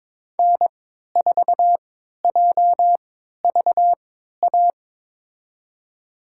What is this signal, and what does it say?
Morse code, heard as a keyed tone, about 22 wpm, spaced as for 13 wpm, 700 Hz. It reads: N4JVA
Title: USA Call Signs - 3x Character Spacing - 22wpm